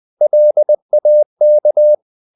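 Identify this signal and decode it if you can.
Morse code, heard as a keyed tone, about 20 wpm, 600 Hz. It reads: LAK
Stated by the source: Morse code practice of short Danish words at 20 wpm, from CW med Gnister.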